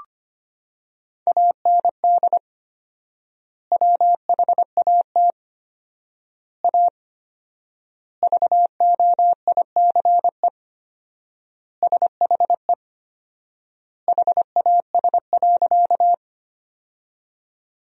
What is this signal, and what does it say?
Morse code, heard as a keyed tone, about 25 wpm, 700 Hz. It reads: AND WHAT A VOICE SHE HAS.
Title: 4x Word-Spacing - Sentences from Top 500 Words - 25wpm